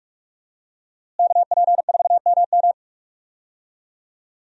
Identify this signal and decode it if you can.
Morse code, heard as a keyed tone, about 46 wpm, 700 Hz. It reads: KP4MM
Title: USA Call Signs - 5 Characters - Repeated 2x - 45wpm